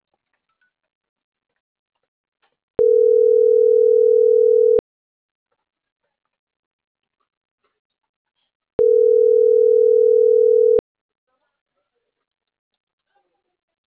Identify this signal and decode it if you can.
Ringback tone (North America), 440 + 480 Hz.